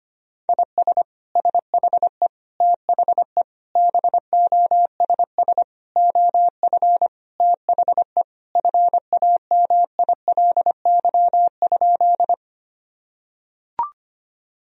Morse code, heard as a keyed tone, about 25 wpm, 700 Hz. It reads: IS SHE THE BOSS OF THE FAMILY?
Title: Sentences from Top 1000 Words - Review Part 4 - 25wpm